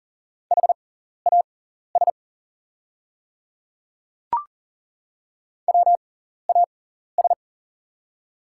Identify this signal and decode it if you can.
Morse code, heard as a keyed tone, about 41 wpm, spaced as for 14 wpm, 700 Hz. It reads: HAS WAS
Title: 6x Character Spacing - Top 100 Words - 40wpm